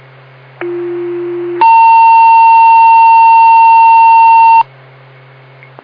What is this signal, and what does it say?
Two-tone page, heard as a one-second tone, then a three-second tone, about 350 then 910 Hz. It is heard from 0.6 to 4.6 s.
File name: Blue Ridge Rescue Squad